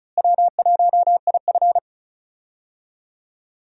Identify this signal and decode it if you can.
Morse code, heard as a keyed tone, about 34 wpm, 700 Hz. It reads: W1IF
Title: USA Call Signs - 4 Characters - Repeated 3x - 35wpm